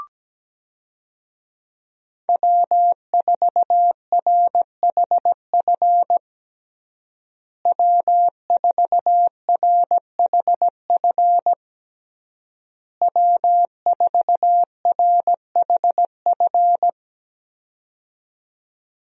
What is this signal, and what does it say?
Morse code, heard as a keyed tone, about 17 wpm, 700 Hz. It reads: W4RHF W4RHF W4RHF
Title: USA Call Signs - 5 Characters - Repeated 3x - 17wpm